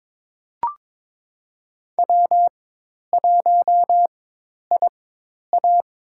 Morse code, heard as a keyed tone, about 23 wpm, spaced as for 10 wpm, 700 Hz. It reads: W1IA
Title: USA Call Signs - 4 Characters - 4x Character Spacing - 22wpm